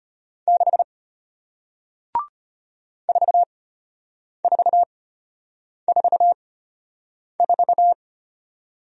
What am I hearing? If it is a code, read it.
Morse code, heard as a keyed tone, about 33 wpm, spaced as for 8 wpm, 700 Hz. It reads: 6 4444